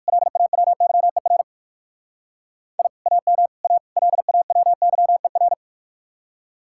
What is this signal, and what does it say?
Morse code, heard as a keyed tone, about 54 wpm, 700 Hz. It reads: LAWYER I AM A LAWYER
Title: Speed-Racing Common QSO Phrases - 45wpm